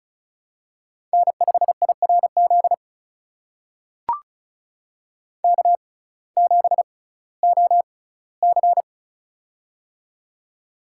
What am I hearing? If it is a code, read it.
Morse code, heard as a keyed tone, about 36 wpm, spaced as for 19 wpm, 700 Hz. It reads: N5IRZ K7OC
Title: USA Call Signs - 6x Character Spacing - 35wpm